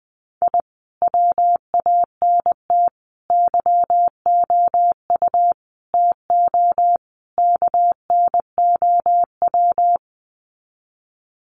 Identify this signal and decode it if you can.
Morse code, heard as a keyed tone, about 20 wpm, 700 Hz. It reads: I WANT YOU TO KNOW
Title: Sets of 5 Words Encore 20wpm